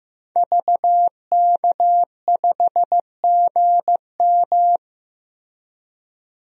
Morse code, heard as a keyed tone, about 15 wpm, 700 Hz. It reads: VK5GM
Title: DXCC-100 Call Signs - 5 Characters - 15wpm